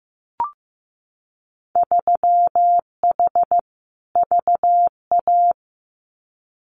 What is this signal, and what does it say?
Morse code, heard as a keyed tone, about 15 wpm, 700 Hz. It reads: 3H VA
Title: Winter Field Day Exchanges 15wpm